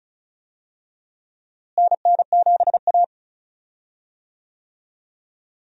Morse code, heard as a keyed tone, about 36 wpm, 700 Hz. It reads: NN7A
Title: USA Call Signs - 4 Characters - Repeated 2x - 35wpm